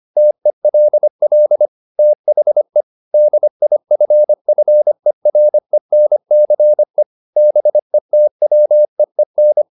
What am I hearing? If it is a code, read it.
Morse code, heard as a keyed tone, about 25 wpm, 600 Hz. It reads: TELL THE DIFFERENCE BETWEEN
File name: the old man and the sea part 1 25wpm